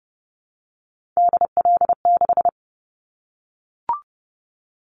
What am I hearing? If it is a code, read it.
Morse code, heard as a keyed tone, about 30 wpm, 690 Hz. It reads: DL6
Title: Callsign Prefixes for Top 25 DXCC - Repeated 3x - 30wpm